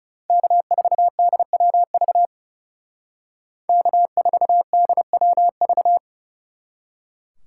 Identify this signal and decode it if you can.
Morse code, heard as a keyed tone, about 37 wpm, 700 Hz. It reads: K4DWV K4DWV
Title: Speed Racing - USA 1x3 Call Signs 30wpm